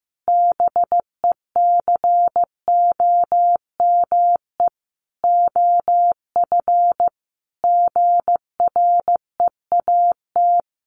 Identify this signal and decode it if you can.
Morse code, heard as a keyed tone, about 15 wpm, 700 Hz. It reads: BECOME OF GREAT
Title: Sentences from Top 1200 Words 15wpm